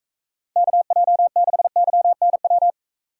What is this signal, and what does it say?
Morse code, heard as a keyed tone, about 40 wpm, 700 Hz. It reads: KJ6YNW